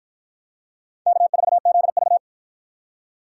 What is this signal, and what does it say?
Morse code, heard as a keyed tone, about 50 wpm, 700 Hz. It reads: K4ZV